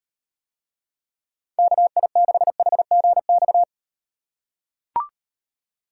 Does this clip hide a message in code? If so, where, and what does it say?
Yes: Morse code from 1.6 to 3.6 s, heard as a keyed tone, about 35 wpm, 700 Hz. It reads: KI6HGX